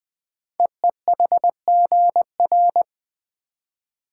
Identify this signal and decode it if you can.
Morse code, heard as a keyed tone, about 20 wpm, 700 Hz. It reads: EEHGR
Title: Code Groups - 5 Letters - 20wpm